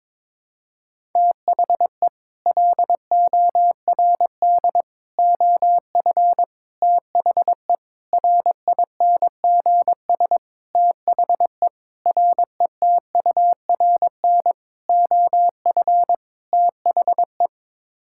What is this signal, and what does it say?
Morse code, heard as a keyed tone, about 22 wpm, 700 Hz. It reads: THE LORD OF THE RINGS THE RETURN OF THE
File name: Sets of 9 Words - 22wpm